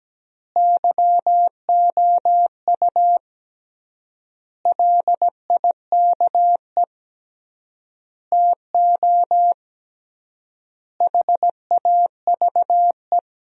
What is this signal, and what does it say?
Morse code, heard as a keyed tone, about 17 wpm, 700 Hz. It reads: YOU LIKE TO HAVE
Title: 3x Word-Spacing - Sentences from Top 100 Words - 17wpm